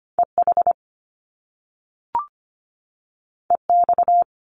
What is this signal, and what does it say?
Morse code, heard as a keyed tone, about 25 wpm, 700 Hz. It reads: EH EX
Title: Top Two Letter Combinations in Words 25wpm